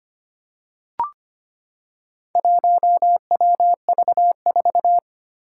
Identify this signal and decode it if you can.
Morse code, heard as a keyed tone, about 25 wpm, 700 Hz. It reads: 1WV4
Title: Four Challenging Letter-Number Combinations 25wpm